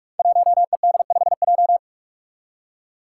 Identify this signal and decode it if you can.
Morse code, heard as a keyed tone, about 44 wpm, 700 Hz. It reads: 1ED5J
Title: Five Challenging Letter-Number Combinations 45wpm